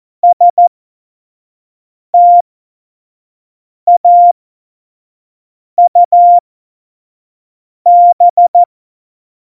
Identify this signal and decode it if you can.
Morse code, heard as a keyed tone, about 14 wpm, spaced as for 5 wpm, 700 Hz. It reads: STAUB
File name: Morsekurs / Witze / Witz 08. August 17, Tempo 25 F70